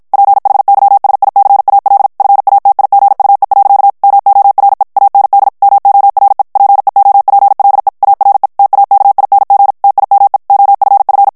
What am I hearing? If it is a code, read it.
Morse code, heard as a keyed tone, about 56 wpm, 800 Hz. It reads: PHYSICAL FATIGUE. MORE AND MORE PEOPLE ARE TAKING TIME OFF
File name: morse mp3 / ANGLAIS texte clair 1 vitesse 50 mots minute